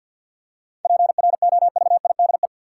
Morse code, heard as a keyed tone, about 47 wpm, 700 Hz. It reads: PROVIDE
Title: Seven Letter Words - 50wpm